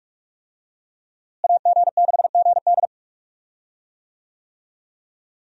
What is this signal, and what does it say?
Morse code, heard as a keyed tone, about 46 wpm, 700 Hz. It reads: AG6GD
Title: USA Call Signs - 5 Characters - Repeated 2x - 45wpm